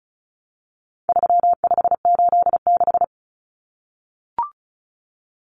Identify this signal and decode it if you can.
Morse code, heard as a keyed tone, about 34 wpm, 700 Hz. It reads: 3586
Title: Numbers - 4-Digit 35wpm